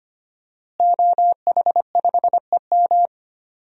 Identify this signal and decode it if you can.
Morse code, heard as a keyed tone, about 25 wpm, 700 Hz. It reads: OH5EM
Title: DXCC-100 Call Signs - 5 Characters - Repeated 3x - 25wpm